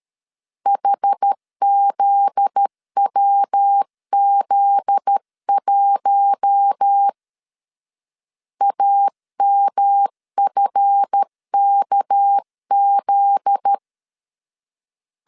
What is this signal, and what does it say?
Morse code, heard as a keyed tone, about 13 wpm, 800 Hz. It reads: HZWZ1 AMFKZ
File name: LEZIONI / 7 GF9Z / 7 GF9Z 12wpm 2147